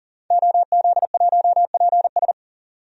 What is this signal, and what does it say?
Morse code, heard as a keyed tone, about 39 wpm, 700 Hz. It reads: OZ1PS
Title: DXCC-100 Call Signs - 5 Characters - 40wpm